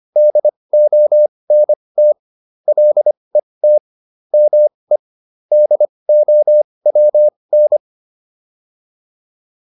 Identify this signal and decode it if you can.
Morse code, heard as a keyed tone, about 25 wpm, spaced as for 20 wpm, 600 Hz. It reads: DONT LET ME DOWN
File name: PR301 20